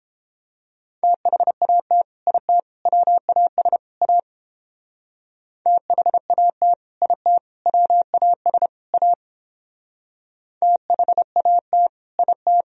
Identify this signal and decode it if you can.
Morse code, heard as a keyed tone, about 29 wpm, 700 Hz. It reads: THAT IT WAS A THAT IT WAS A THAT IT